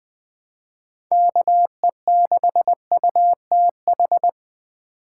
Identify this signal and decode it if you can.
Morse code, heard as a keyed tone, about 20 wpm, 700 Hz. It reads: KE6UTH
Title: USA Call Signs - 6 Characters - 4x Character Spacing - 20wpm